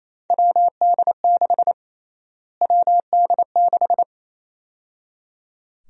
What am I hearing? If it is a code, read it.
Morse code, heard as a keyed tone, about 28 wpm, 700 Hz. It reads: WD6 WD6
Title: USA Call Sign Prefixes - Repeated 2x - 28wpm